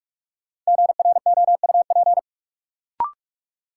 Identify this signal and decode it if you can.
Morse code, heard as a keyed tone, about 46 wpm, 700 Hz. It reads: GROUP
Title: Sentences from Top 1000 Words - Review Part 1 - New Word Drill - 45wpm